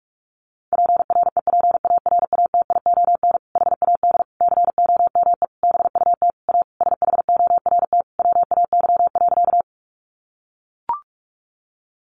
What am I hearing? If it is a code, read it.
Morse code, heard as a keyed tone, about 44 wpm, 700 Hz. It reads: PREPARATION HAD COME BUT A SHORT WAY.
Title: Sentences from Top 1400 Words 45wpm